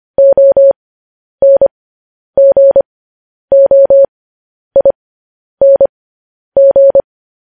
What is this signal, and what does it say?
Morse code, heard as a keyed tone, about 25 wpm, spaced as for 10 wpm, 550 Hz. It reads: ONGOING
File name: podcast / News headlines-1x-25x10